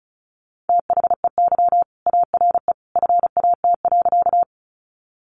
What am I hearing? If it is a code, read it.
Morse code, heard as a keyed tone, about 34 wpm, 700 Hz. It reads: THEY ARE FAT.
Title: Sentences from Top 600 Words 35wpm